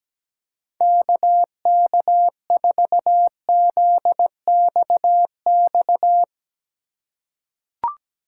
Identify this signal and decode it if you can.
Morse code, heard as a keyed tone, about 17 wpm, 700 Hz. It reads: KK4ZXX
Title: USA Call Signs - 6 Characters - 3x Character Spacing - 17wpm